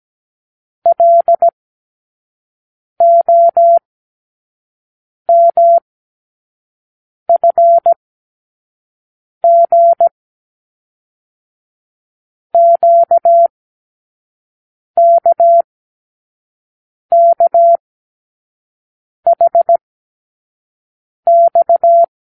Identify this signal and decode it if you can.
Morse code, heard as a keyed tone, about 17 wpm, spaced as for 5 wpm, 680 Hz. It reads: LOMFG QKKHX
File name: k7qo / Lesson 36 (A-X)